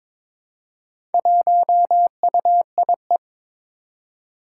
Morse code, heard as a keyed tone, about 22 wpm, 700 Hz. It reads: 1UIE
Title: Four Challenging Letter-Number Combinations 22wpm